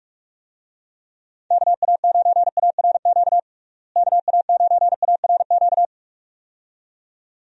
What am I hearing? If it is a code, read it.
Morse code, heard as a keyed tone, about 46 wpm, 700 Hz. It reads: KA9ARQ KA9ARQ
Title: USA Call Signs - 6 Characters - Repeated 2x - 45wpm